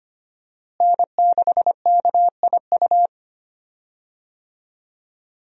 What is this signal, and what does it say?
Morse code, heard as a keyed tone, about 25 wpm, 700 Hz. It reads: N6KIU
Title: USA Call Signs - 5 Characters - Repeated 3x - 25wpm